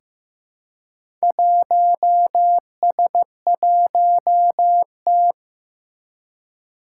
Morse code, heard as a keyed tone, about 15 wpm, 700 Hz. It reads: 1S1T
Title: Four Challenging Letter-Number Combinations 15wpm